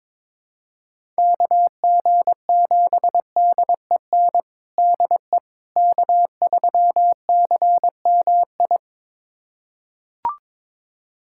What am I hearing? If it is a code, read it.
Morse code, heard as a keyed tone, about 22 wpm, 700 Hz. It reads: KG7DEN DE K3CMI
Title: USA Callsign DE Callsign 22wpm